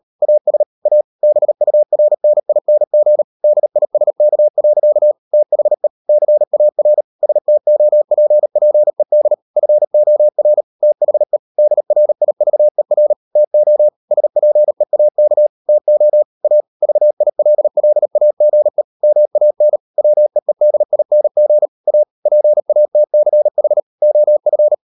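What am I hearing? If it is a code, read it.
Morse code, heard as a keyed tone, about 38 wpm, 600 Hz. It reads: AS A BURNING DISK. THE CAR STOPPED FOR THE DRIVER TO SPEAK TO A VILLAGE MAN WEEDING A PATCH OF
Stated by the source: Morse code practice at 35–39 wpm sending English text of Ken Follett's novel Never, from CW med Gnister.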